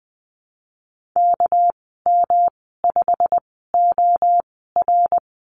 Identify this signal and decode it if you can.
Morse code, heard as a keyed tone, about 20 wpm, spaced as for 15 wpm, 700 Hz. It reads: KM5OR
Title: 207 Lesson 33g - Letter j - Call Signs 20wpm